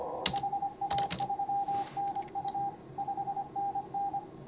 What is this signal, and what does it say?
Morse code, heard as a keyed tone, about 24 wpm, 810 Hz, with noise. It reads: HA3NU 5NN
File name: CW / HA5MA RHOB HANG CW 2017